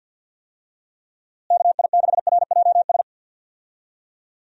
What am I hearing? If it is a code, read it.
Morse code, heard as a keyed tone, about 48 wpm, 700 Hz. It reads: KI6RJS
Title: USA Call Signs - 6 Characters - Repeated 2x - 50wpm